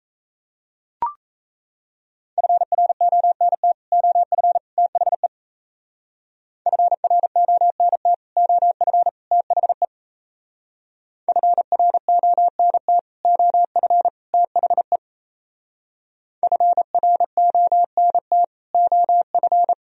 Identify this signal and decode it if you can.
Morse code, heard as a keyed tone, about 37 wpm, 700 Hz. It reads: FRONT OF THE FRONT OF THE FRONT OF THE FRONT OF